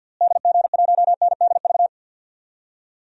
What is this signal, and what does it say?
Morse code, heard as a keyed tone, about 52 wpm, 700 Hz. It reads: DG1NDV
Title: DXCC-100 Call Signs - 6 Characters - Repeated 3x - 50wpm